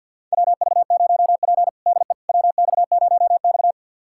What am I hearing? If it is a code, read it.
Morse code, heard as a keyed tone, about 51 wpm, 690 Hz. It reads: WV0P DE WX0X
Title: USA Callsign DE Callsign - 4 Characters - 50wpm